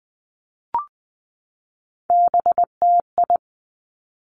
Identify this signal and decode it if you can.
Morse code, heard as a keyed tone, about 20 wpm, 700 Hz. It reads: BTI
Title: Three Challenging Letter-Number Combinations 20wpm